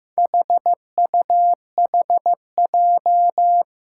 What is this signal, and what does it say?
Morse code, heard as a keyed tone, about 15 wpm, 700 Hz. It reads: HUHJ
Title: Four Challenging Letter-Number Combinations 15wpm